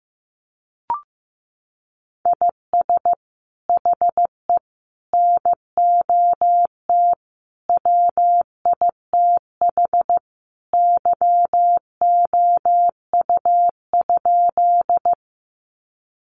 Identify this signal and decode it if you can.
Morse code, heard as a keyed tone, about 15 wpm, 700 Hz. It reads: IS HE NOT WITH YOU?
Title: Sentences from Top 600 Words 15wpm